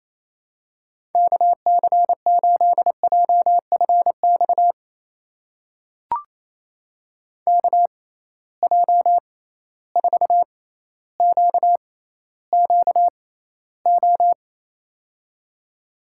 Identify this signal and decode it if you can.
Morse code, heard as a keyed tone, about 28 wpm, spaced as for 14 wpm, 700 Hz. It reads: KC8JFX KJ4QQO
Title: USA Call Signs - 6 Characters - 6x Character Spacing - 28wpm